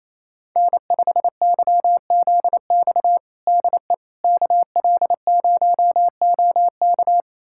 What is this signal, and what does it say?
Morse code, heard as a keyed tone, about 28 wpm, 700 Hz. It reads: N5YZX DE KL0OK
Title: USA Callsign DE Callsign - 5 Characters - 28wpm